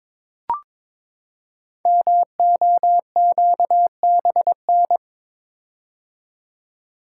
Morse code, heard as a keyed tone, about 22 wpm, 700 Hz. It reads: MOQBN
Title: Code Groups - 5 Letters - 22wpm